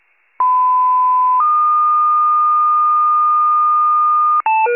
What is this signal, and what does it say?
Two-tone page, heard as a one-second tone, then a three-second tone, about 1010 then 1220 Hz. It is heard from 0.4 to 4.4 s.
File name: pager / 94 RESCUE OFC / wcalert